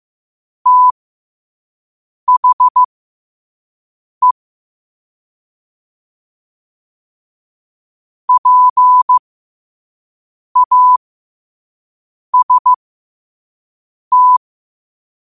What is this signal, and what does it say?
Morse code, heard as a keyed tone, about 15 wpm, spaced as for 5 wpm, 1000 Hz. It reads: THE PAST
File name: qotdfiles / 5wpm